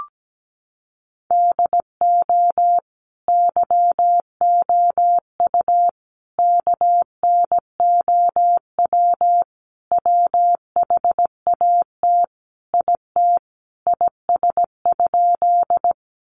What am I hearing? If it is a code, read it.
Morse code, heard as a keyed tone, about 17 wpm, 700 Hz. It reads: DO YOU KNOW WHAT IT IS?
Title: Sentences from Top 100 Words 17wpm